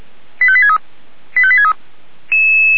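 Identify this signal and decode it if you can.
Five-tone selective call (ZVEI), heard as quick runs of five tones, tones of about 70 ms. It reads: 85863 85863
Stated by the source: Five-tone ZVEI selective call received on an SDR.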